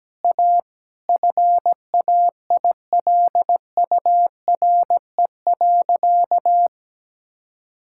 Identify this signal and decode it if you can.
Morse code, heard as a keyed tone, about 17 wpm, 700 Hz. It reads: A FAILURE.
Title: Sentences from Top 1000 Words - Review Part 6 - 17wpm